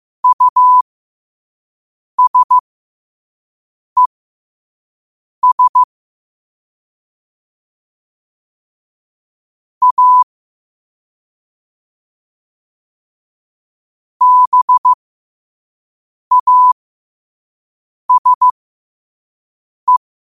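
Morse code, heard as a keyed tone, about 15 wpm, spaced as for 5 wpm, 1000 Hz. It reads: USES A BASE
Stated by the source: Morse code sent at 5 wpm for learning.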